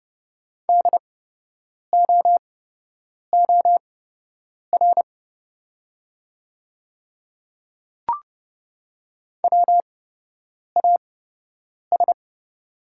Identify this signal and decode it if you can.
Morse code, heard as a keyed tone, about 30 wpm, spaced as for 8 wpm, 690 Hz. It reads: DOOR WAS
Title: 8x Character Spacing - Top 500 Words - 30wpm